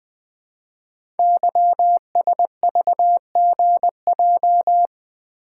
Morse code, heard as a keyed tone, about 20 wpm, 700 Hz. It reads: YSVGJ
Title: Code Groups - 5 Letters - 20wpm